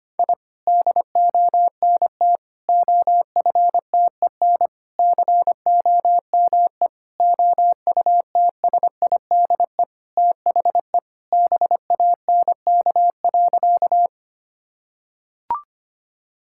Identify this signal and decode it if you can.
Morse code, heard as a keyed tone, about 25 wpm, 700 Hz. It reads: I DONT OFTEN COME OUTSIDE THE BANK.
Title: Sentences from Top 1000 Words - Review Part 6 - 25wpm